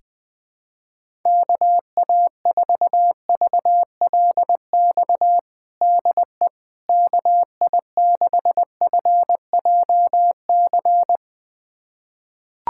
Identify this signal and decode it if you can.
Morse code, heard as a keyed tone, about 20 wpm, 700 Hz. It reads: KA4VLX DE KI6FJC
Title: USA Callsign DE Callsign - 6 Characters - 20wpm